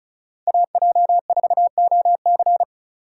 Morse code, heard as a keyed tone, about 34 wpm, 700 Hz. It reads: AJ4OC
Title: USA Call Signs - 5 Characters - 4x Character Spacing - 35wpm